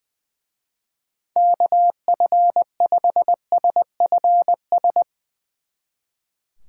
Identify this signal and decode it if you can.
Morse code, heard as a keyed tone, about 22 wpm, 700 Hz. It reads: KF5SFS
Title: Speed Racing - USA 2x3 Call Signs 20wpm